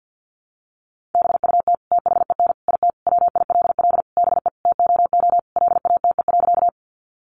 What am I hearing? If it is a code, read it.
Morse code, heard as a keyed tone, about 50 wpm, 700 Hz. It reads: BUT THEN IT WILL BE TOO LATE.
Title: Sentences from Top 300 Words 50wpm